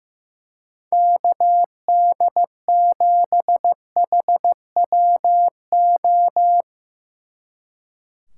Morse code, heard as a keyed tone, about 16 wpm, 700 Hz. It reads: KD7HWO
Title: Speed Racing - USA 2x3 Call Signs 15wpm